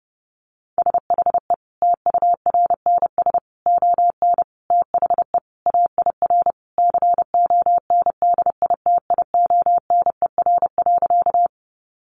Morse code, heard as a keyed tone, about 30 wpm, 700 Hz. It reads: SHE TURNS ON THE AIR CONDITIONER.
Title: Easy Phrases 30wpm